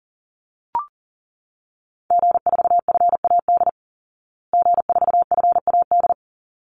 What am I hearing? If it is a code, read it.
Morse code, heard as a keyed tone, about 40 wpm, 700 Hz. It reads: G4FAD G4FAD
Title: Call Signs Repeated 3x - 40wpm